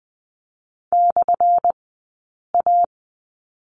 Morse code, heard as a keyed tone, about 20 wpm, spaced as for 8 wpm, 700 Hz. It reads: /A
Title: 177 Lesson 29d - Slash - Two Characters 20wpm